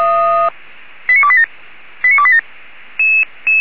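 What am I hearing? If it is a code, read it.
Five-tone selective call (ZVEI), heard as quick runs of five tones, tones of about 70 ms. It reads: 89287 89287
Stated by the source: ZVEI five-tone selective call received by SDR.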